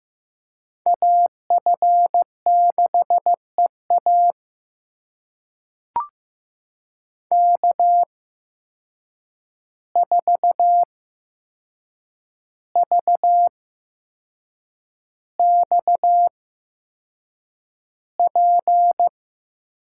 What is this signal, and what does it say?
Morse code, heard as a keyed tone, about 15 wpm, spaced as for 6 wpm, 700 Hz. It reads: AF6EA K4VXP
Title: USA Call Signs - 5 Characters - 8x Character Spacing - 15wpm